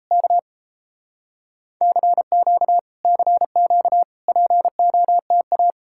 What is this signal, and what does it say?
Morse code, heard as a keyed tone, about 39 wpm, 700 Hz. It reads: K CQ CQ POTA